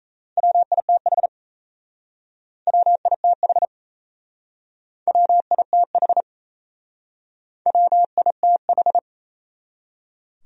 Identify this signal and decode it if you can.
Morse code, heard as a keyed tone, about 36 wpm, 700 Hz. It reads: WITH WITH WITH WITH